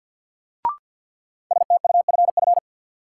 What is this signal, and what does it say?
Morse code, heard as a keyed tone, about 51 wpm, 700 Hz. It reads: STUFF
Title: Sentences from Top 1400 Words - New Word Drill - 50wpm